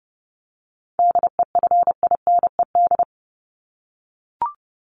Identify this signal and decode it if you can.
Morse code, heard as a keyed tone, about 30 wpm, 700 Hz. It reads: DEFINED
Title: ED Suffix - Top 100 Words with Base Words 30wpm